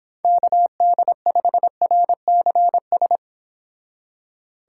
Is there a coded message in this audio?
Yes: Morse code from 0.2 to 3.2 s, heard as a keyed tone, about 26 wpm, 700 Hz. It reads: KD5RCS